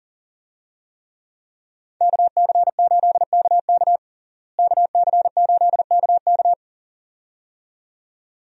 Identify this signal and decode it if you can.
Morse code, heard as a keyed tone, about 40 wpm, 700 Hz. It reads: KC8KK KC8KK
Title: USA Call Signs - 5 Characters - Repeated 2x - 40wpm